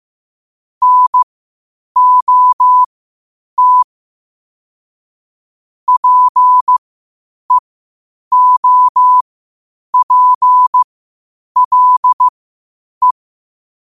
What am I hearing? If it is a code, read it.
Morse code, heard as a keyed tone, about 15 wpm, spaced as for 8 wpm, 1000 Hz. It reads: NOT PEOPLE
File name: qotdfiles / 8wpm